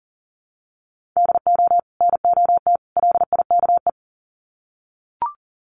Morse code, heard as a keyed tone, about 40 wpm, 700 Hz. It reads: DO NOT LIKE
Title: Sets of 3 Words 40wpm